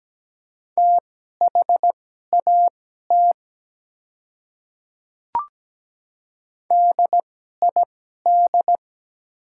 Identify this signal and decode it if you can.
Morse code, heard as a keyed tone, about 17 wpm, spaced as for 12 wpm, 700 Hz. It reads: THAT DID
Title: 2x Character Spacing - Top 200 Words - 17wpm